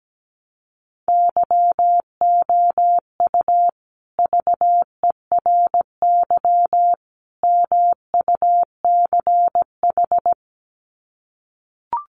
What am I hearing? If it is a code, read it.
Morse code, heard as a keyed tone, about 17 wpm, 700 Hz. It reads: YOU VERY MUCH
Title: Sets of 3 Words 17wpm